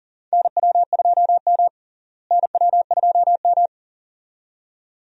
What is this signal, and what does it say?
Morse code, heard as a keyed tone, about 40 wpm, 700 Hz. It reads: NW2M NW2M
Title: USA Call Signs - 4 Characters - Repeated 2x - 40wpm